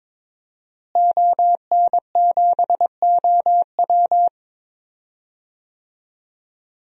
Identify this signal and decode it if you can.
Morse code, heard as a keyed tone, about 21 wpm, 700 Hz. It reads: ON7OW
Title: DXCC-100 Call Signs - 5 Characters - Repeated 3x - 22wpm